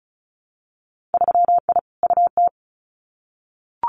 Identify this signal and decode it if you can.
Morse code, heard as a keyed tone, about 36 wpm, 700 Hz. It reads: 3I UT
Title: Winter Field Day Exchanges Encore 35wpm